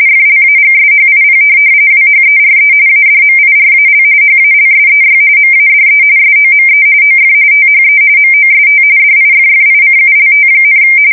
RTTY, about 45.45 baud. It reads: FLAGSHIP WHERE ADMIRAL HALSEY MADE SOME OF THE MOST DIFFICULT,
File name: fsk170 / USS-New-Jersey